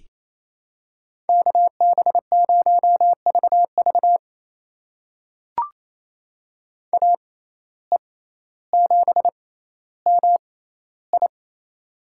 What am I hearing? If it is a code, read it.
Morse code, heard as a keyed tone, about 28 wpm, spaced as for 14 wpm, 700 Hz. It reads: KB0VV AE7MI
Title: USA Call Signs - 5 Characters - 6x Character Spacing - 28wpm